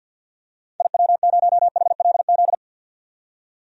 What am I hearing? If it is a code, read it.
Morse code, heard as a keyed tone, about 51 wpm, 700 Hz. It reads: IW0HLZ
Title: DXCC-100 Call Signs - 6 Characters - 50wpm